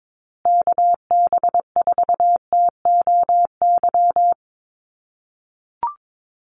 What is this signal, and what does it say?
Morse code, heard as a keyed tone, about 22 wpm, 700 Hz. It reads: KB4TOY
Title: USA Call Signs Repeated 2x 22wpm